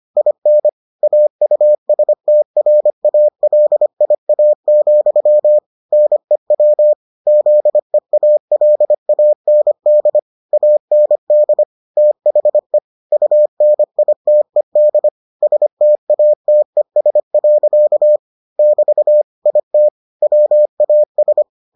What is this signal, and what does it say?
Morse code, heard as a keyed tone, about 25 wpm, 600 Hz. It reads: IN AUSTRALIA, NEW ZEALAND AND THE UNITED STATES. <BT> IT WAS